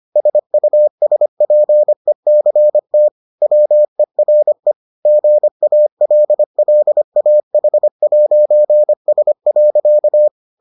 Morse code, heard as a keyed tone, about 25 wpm, 600 Hz. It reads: SUSPECT WERE GALLAH'S.